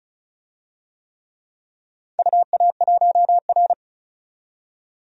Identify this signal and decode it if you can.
Morse code, heard as a keyed tone, about 36 wpm, 700 Hz. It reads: UA1R
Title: DXCC-100 Call Signs - 4 Characters - 35wpm